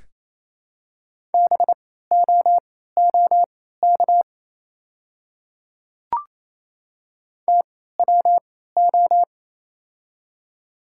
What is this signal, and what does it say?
Morse code, heard as a keyed tone, about 27 wpm, spaced as for 16 wpm, 700 Hz. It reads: BOOK TWO
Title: 3x Character Spacing - Top 300 Words - 28wpm